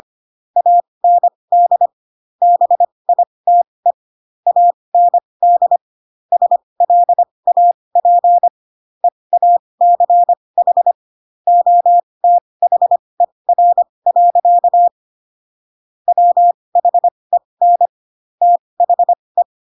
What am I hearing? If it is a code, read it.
Morse code, heard as a keyed tone, about 25 wpm, spaced as for 20 wpm, 700 Hz. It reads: AND BITE AND SLAP EACH OTHER. WHEN THE
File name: Pinocchio / Pinocchio ch2 15 25 ik1hge